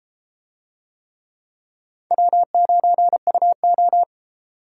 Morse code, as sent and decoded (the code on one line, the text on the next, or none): .-- ----. ..- ---
W9UO